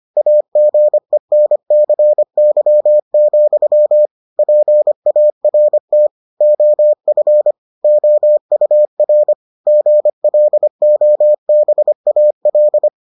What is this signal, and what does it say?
Morse code, heard as a keyed tone, about 25 wpm, 600 Hz. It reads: AGENCY, PART OF OUR GLOBAL